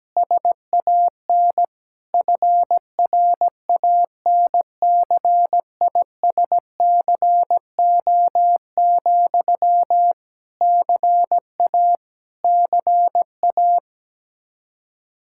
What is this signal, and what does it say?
Morse code, heard as a keyed tone, about 17 wpm, 700 Hz. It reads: SAN FRANCISCO, CA CA
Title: Practice QSO - Exchange 1 - Beginning - 17wpm